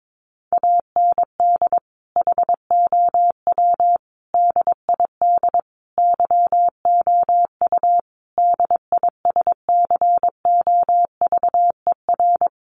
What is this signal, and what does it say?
Morse code, heard as a keyed tone, about 22 wpm, 700 Hz. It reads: AND HOW DID YOU DISCOVER
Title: Sentences from Top 1100 Words 22wpm